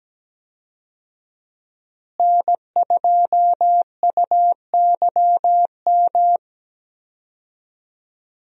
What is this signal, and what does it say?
Morse code, heard as a keyed tone, about 17 wpm, 700 Hz. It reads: N2UYM
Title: USA Call Signs - 5 Characters - Repeated 2x - 17wpm